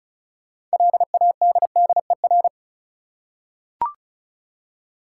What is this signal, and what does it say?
Morse code, heard as a keyed tone, about 36 wpm, 700 Hz. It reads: LADDER
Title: Six Letter Words - 35wpm